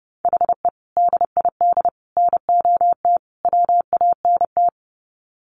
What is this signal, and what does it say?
Morse code, heard as a keyed tone, about 30 wpm, 700 Hz. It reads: HE DID NOT WANT
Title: Sets of 4 Words 30wpm